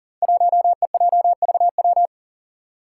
Morse code, heard as a keyed tone, about 41 wpm, 700 Hz. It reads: 1EJVW
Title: Five Challenging Letter-Number Combinations 40wpm